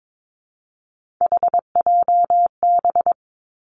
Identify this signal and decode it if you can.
Morse code, heard as a keyed tone, about 22 wpm, 700 Hz. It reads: HJB
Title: Three Challenging Letter-Number Combinations 22wpm